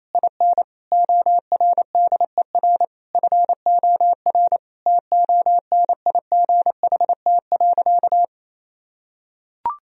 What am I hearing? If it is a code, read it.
Morse code, heard as a keyed tone, about 28 wpm, 700 Hz. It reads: IN ORDER FOR TONIGHT.
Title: Sentences from Top 1000 Words - Review Part 3 - 28wpm